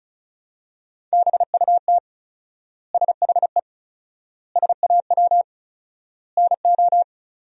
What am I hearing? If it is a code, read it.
Morse code, heard as a keyed tone, about 36 wpm, 700 Hz. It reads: BUT SHE SAW NO
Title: 4x Word-Spacing - Sentences from Top 300 Words - 35wpm